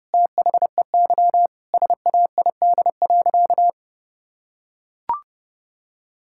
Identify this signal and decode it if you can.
Morse code, heard as a keyed tone, about 30 wpm, 700 Hz. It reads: THEY SAID.
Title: Sentences from Top 1600 Words 30wpm